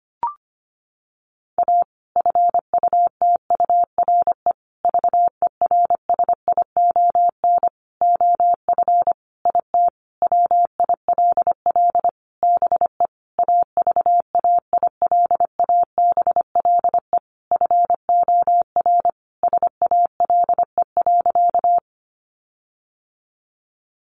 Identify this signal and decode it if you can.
Morse code, heard as a keyed tone, about 25 wpm, 700 Hz. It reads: A FUTURE VERSION OF IT WILL BE AVAILABLE FOR SALE.
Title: Sentences from Top 1400 Words 25wpm